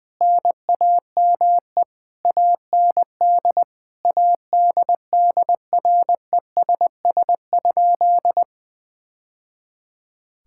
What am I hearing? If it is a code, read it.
Morse code, heard as a keyed tone, about 20 wpm, 700 Hz. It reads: NAME AND ADDRESS?
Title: Sentences from Top 1000 Words - Review Part 6 - 20wpm